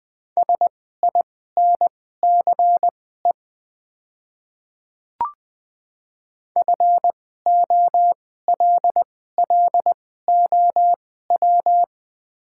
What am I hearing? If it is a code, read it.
Morse code, heard as a keyed tone, about 20 wpm, spaced as for 15 wpm, 700 Hz. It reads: SINCE FOLLOW
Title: 2x Character Spacing - Top 500 Words - 20wpm